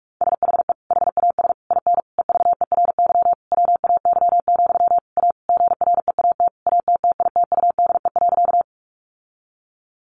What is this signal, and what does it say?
Morse code, heard as a keyed tone, about 44 wpm, 700 Hz. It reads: SHE HAS IN EVERY WAY, A GREAT ATTITUDE.
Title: Sentences from Top 1300 Words 45wpm